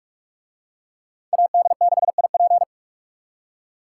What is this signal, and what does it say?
Morse code, heard as a keyed tone, about 44 wpm, 700 Hz. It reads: AD6IP